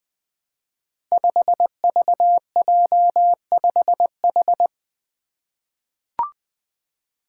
Morse code, heard as a keyed tone, about 20 wpm, 700 Hz. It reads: 5VJ5H
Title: Five Challenging Letter-Number Combinations 20wpm